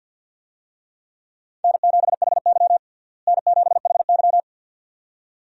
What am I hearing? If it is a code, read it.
Morse code, heard as a keyed tone, about 53 wpm, 700 Hz. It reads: N7HY N7HY
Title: USA Call Signs - 4 Characters - Repeated 2x - 50wpm